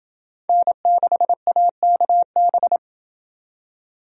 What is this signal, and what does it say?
Morse code, heard as a keyed tone, about 29 wpm, 700 Hz. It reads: N6AKB